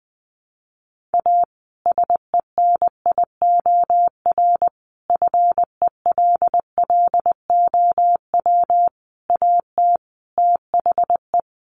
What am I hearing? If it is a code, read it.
Morse code, heard as a keyed tone, about 20 wpm, 700 Hz. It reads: A SENIOR FELLOW AT THE
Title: Sets of 5 Words Encore 20wpm